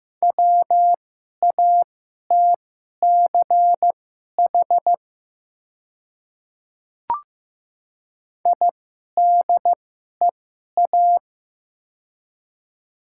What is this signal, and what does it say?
Morse code, heard as a keyed tone, about 15 wpm, spaced as for 11 wpm, 700 Hz. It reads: WATCH IDEA
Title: 2x Character Spacing - Top 300 Words - 15wpm